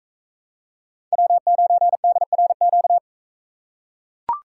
Morse code, heard as a keyed tone, about 41 wpm, 700 Hz. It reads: W9DRQ